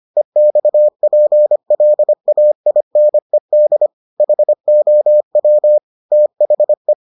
Morse code, heard as a keyed tone, about 25 wpm, 600 Hz. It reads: EXPLAINED HOW THE